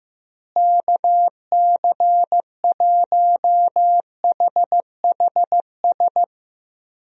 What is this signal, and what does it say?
Morse code, heard as a keyed tone, about 15 wpm, 700 Hz. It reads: KC1HHS
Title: USA Call Signs - 6 Characters - Repeated 3x - 15wpm